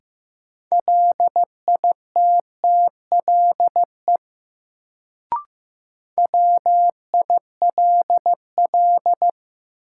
Morse code, heard as a keyed tone, about 16 wpm, 700 Hz. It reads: LITTLE WILL
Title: Sets of 2 Words - New Word Drill - 15wpm